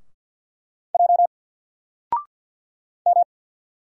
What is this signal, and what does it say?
Morse code, heard as a keyed tone, about 50 wpm, 700 Hz. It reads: J M
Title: Single Letter-Number 50wpm